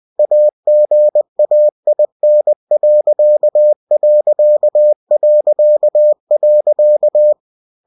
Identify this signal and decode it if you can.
Morse code, heard as a keyed tone, about 20 wpm, 600 Hz. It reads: AGAIN....